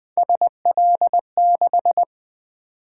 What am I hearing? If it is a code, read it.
Morse code, heard as a keyed tone, about 20 wpm, 700 Hz. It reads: SL6
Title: Three Letter-Number Combinations - 20wpm